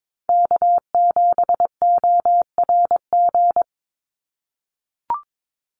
Morse code, heard as a keyed tone, about 22 wpm, 700 Hz. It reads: K7ORG
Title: NA SOTA Chaser Callsigns 22wpm